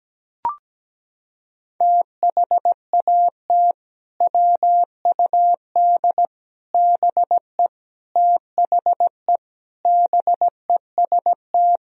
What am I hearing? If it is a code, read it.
Morse code, heard as a keyed tone, about 17 wpm, 700 Hz. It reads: THAT WUD BE THE BEST
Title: Sentences from Top 400 Words and Abbreviations 17wpm